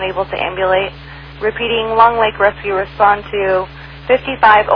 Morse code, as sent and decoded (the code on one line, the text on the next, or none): none